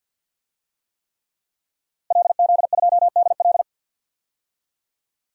Morse code, heard as a keyed tone, about 47 wpm, 700 Hz. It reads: LZ2DL
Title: DXCC-100 Call Signs - 5 Characters - Repeated 3x - 50wpm